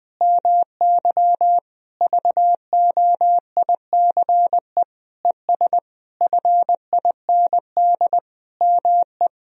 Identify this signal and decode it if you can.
Morse code, heard as a keyed tone, about 20 wpm, 700 Hz. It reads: MY VOICE ES FIND ME
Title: Sentences from Top 500 Words and Abbreviations 20wpm